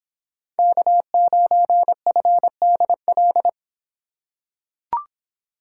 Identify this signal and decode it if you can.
Morse code, heard as a keyed tone, about 26 wpm, 700 Hz. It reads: K9FDL